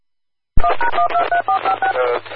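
DTMF, heard as telephone touch tones, tones of about 110 ms. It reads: 1#123456
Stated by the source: DTMF heard on Austin, TX amateur radio repeaters.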